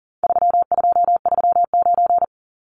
Morse code, heard as a keyed tone, about 40 wpm, 700 Hz. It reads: 3239
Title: Numbers - 4-Digit 40wpm